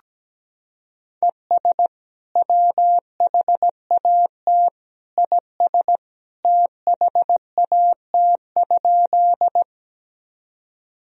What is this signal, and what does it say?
Morse code, heard as a keyed tone, about 17 wpm, 700 Hz. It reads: ES WHAT IS THAT?
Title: Sentences from Top 100 Words and Abbreviations 17wpm